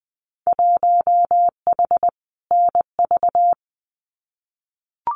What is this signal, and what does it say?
Morse code, heard as a keyed tone, about 20 wpm, 700 Hz. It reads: 1H NV
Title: Winter Field Day Exchanges 20wpm